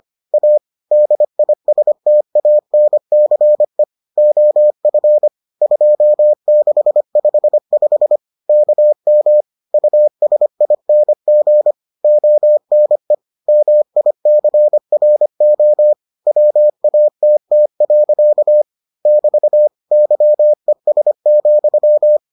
Morse code, heard as a keyed tone, about 25 wpm, 600 Hz. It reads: A DISTANCE OF 2655 KM USING ONE MICRO WATT. <BT> YES,